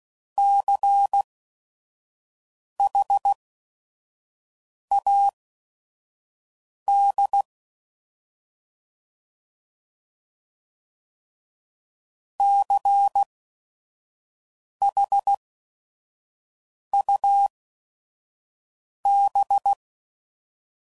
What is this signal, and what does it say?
Morse code, heard as a keyed tone, about 16 wpm, spaced as for 5 wpm, 800 Hz. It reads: CHAD CHUB
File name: Prefixes / By First Letter / Cs